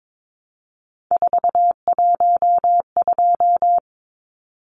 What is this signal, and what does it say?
Morse code, heard as a keyed tone, about 22 wpm, 700 Hz. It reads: 412
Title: Numbers - 3-Digit 22wpm